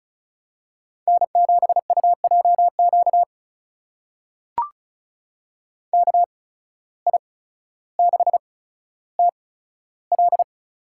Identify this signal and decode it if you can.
Morse code, heard as a keyed tone, about 36 wpm, spaced as for 15 wpm, 700 Hz. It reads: N7UJQ KI6TL
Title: USA Call Signs - 5 Characters - 8x Character Spacing - 35wpm